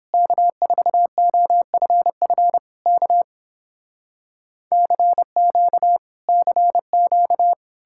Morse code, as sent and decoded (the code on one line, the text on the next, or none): -.- ....- --- ..-. ..-. / -.- / -.-. --.- / -.-. --.-
K4OFF K CQ CQ